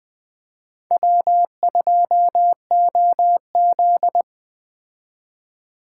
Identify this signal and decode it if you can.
Morse code, heard as a keyed tone, about 20 wpm, 700 Hz. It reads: W2OZ